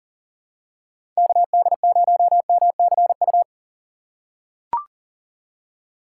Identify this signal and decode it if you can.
Morse code, heard as a keyed tone, about 39 wpm, 700 Hz. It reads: KD0MCU
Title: USA Call Signs - 6 Characters - 4x Character Spacing - 40wpm